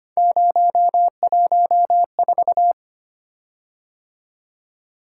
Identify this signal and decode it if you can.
Morse code, heard as a keyed tone, about 25 wpm, 700 Hz. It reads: 014
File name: Three Letter-Number Combinations - 25wpm